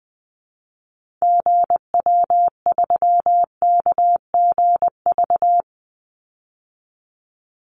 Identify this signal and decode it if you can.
Morse code, heard as a keyed tone, about 20 wpm, 700 Hz. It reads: GW3KGV
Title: Call Signs Repeated 3x - 20wpm